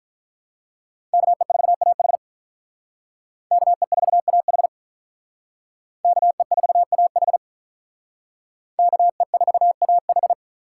Morse code, heard as a keyed tone, about 44 wpm, 700 Hz. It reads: KE4AH KE4AH KE4AH KE4AH